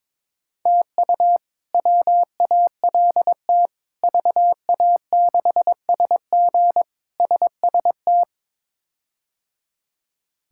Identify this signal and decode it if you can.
Morse code, heard as a keyed tone, about 22 wpm, 700 Hz. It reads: TU WALT VA6SG SST
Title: K1USN SST QSO Practice - 22wpm